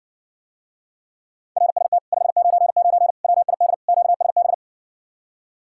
Morse code, heard as a keyed tone, about 60 wpm, 700 Hz. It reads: RST 599 WID QSB